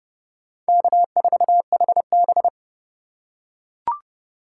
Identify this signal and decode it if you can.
Morse code, heard as a keyed tone, about 30 wpm, 700 Hz. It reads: K4HB